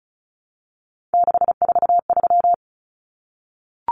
Morse code, heard as a keyed tone, about 34 wpm, 700 Hz. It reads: 643
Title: Common Cut Numbers - 3-Digit 35wpm